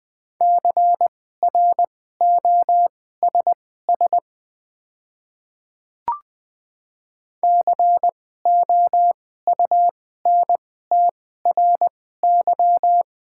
Morse code, heard as a keyed tone, about 20 wpm, spaced as for 15 wpm, 700 Hz. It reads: CROSS COUNTRY
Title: 2x Character Spacing - Top 300 Words - 20wpm